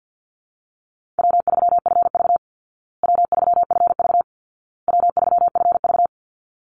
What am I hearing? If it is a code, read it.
Morse code, heard as a keyed tone, about 49 wpm, 700 Hz. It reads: W3FV W3FV W3FV
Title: Easy Call Signs Repeated 3x - 50wpm